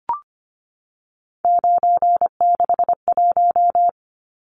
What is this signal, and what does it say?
Morse code, heard as a keyed tone, about 25 wpm, 700 Hz. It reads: 961
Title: Numbers - 3-Digit 25wpm